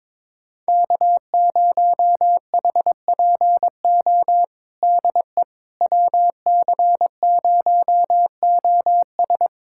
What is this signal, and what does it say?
Morse code, heard as a keyed tone, about 22 wpm, 700 Hz. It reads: K0HPO DE WC0OS
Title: USA Callsign DE Callsign - 5 Characters - 22wpm